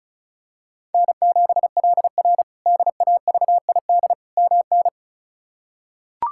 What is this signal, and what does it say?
Morse code, heard as a keyed tone, about 36 wpm, 700 Hz. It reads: N7LR DAVID MN
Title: K1USN SST QSO Practice - 35wpm